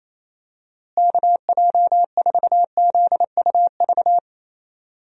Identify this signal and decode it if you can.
Morse code, heard as a keyed tone, about 28 wpm, 700 Hz. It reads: KJ4ZUV
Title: USA Call Signs - 6 Characters - Repeated 2x - 28wpm